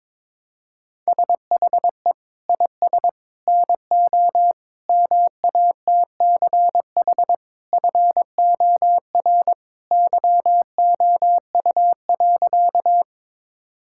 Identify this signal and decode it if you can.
Morse code, heard as a keyed tone, about 22 wpm, 700 Hz. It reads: SHE IS NO MATCH FOR YOU.
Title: Sentences from Top 1600 Words 22wpm